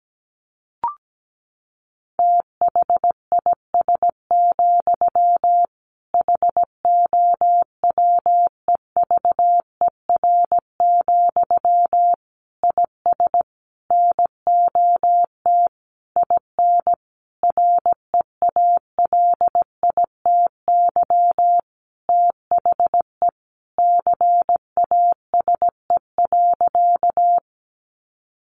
Morse code, heard as a keyed tone, about 17 wpm, 700 Hz. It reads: THIS, HOWEVER, IS NOT IN REALITY THE CASE.
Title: Sentences from Top 1200 Words 17wpm